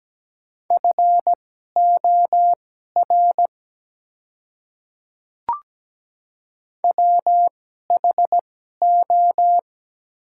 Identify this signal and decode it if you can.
Morse code, heard as a keyed tone, about 17 wpm, spaced as for 12 wpm, 700 Hz. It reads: FOR WHO
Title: 2x Character Spacing - Top 100 Words - 17wpm